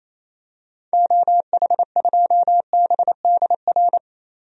Morse code, heard as a keyed tone, about 27 wpm, 700 Hz. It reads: OH2BDR
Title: DXCC-100 Call Signs - 6 Characters - Repeated 2x - 28wpm